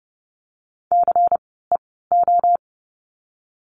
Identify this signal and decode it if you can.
Morse code, heard as a keyed tone, about 31 wpm, 700 Hz. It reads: C E O
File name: Lesson 16f - Letter c - Words 30wpm